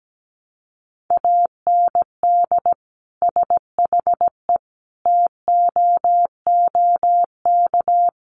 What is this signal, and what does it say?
Morse code, heard as a keyed tone, about 17 wpm, 700 Hz. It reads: AND SHE TOOK
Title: Sentences from Top 1200 Words 17wpm